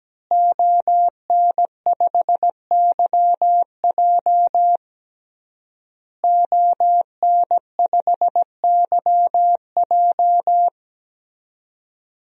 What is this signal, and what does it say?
Morse code, heard as a keyed tone, about 17 wpm, 700 Hz. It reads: ON5YJ ON5YJ
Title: DXCC-100 Call Signs - 5 Characters - Repeated 3x - 17wpm